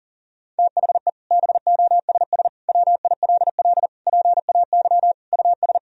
Morse code, heard as a keyed tone, about 40 wpm, 700 Hz. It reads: THE BOSS WILL PAY US
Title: Sentences from Top 1000 Words - Review Part 4 - 40wpm